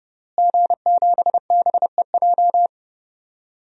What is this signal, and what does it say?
Morse code, heard as a keyed tone, about 31 wpm, 700 Hz. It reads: G7BEJ
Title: DXCC-100 Call Signs - 5 Characters - Repeated 2x - 30wpm